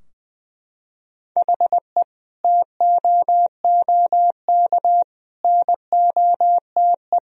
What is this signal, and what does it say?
Morse code, heard as a keyed tone, about 20 wpm, 700 Hz. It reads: HE TOOK NOTE
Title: Sentences from Top 1000 Words - Review Part 2 - 20wpm